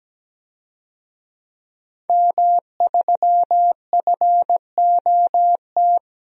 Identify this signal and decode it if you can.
Morse code, heard as a keyed tone, about 17 wpm, 700 Hz. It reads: M3FOT
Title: DXCC-100 Call Signs - 5 Characters - Repeated 3x - 17wpm